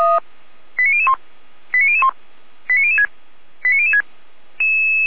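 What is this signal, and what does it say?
Five-tone selective call (ZVEI), heard as quick runs of five tones, tones of about 70 ms. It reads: 89001 89001 89006 89006
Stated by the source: ZVEI five-tone call received by SDR, from the feuerwehr-orlamuende.org audio archive.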